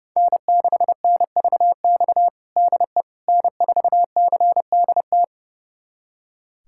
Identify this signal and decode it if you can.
Morse code, heard as a keyed tone, about 30 wpm, 700 Hz. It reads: N6NVX DE N4CDT
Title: USA Callsign DE Callsign - 5 Characters - 30wpm